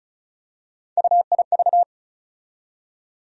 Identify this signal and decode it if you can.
Morse code, heard as a keyed tone, about 36 wpm, 700 Hz. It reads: UIV
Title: USA Call Sign Postfixes - 3 Letters 35wpm